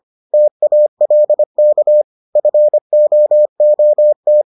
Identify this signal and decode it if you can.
Morse code, heard as a keyed tone, about 25 wpm, 600 Hz. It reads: TALK FOOT